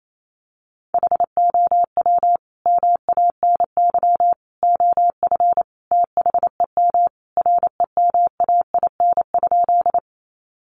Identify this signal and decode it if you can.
Morse code, heard as a keyed tone, about 28 wpm, 700 Hz. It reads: HOW MANY OF THEM REMAIN?
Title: Sentences from Top 1200 Words 28wpm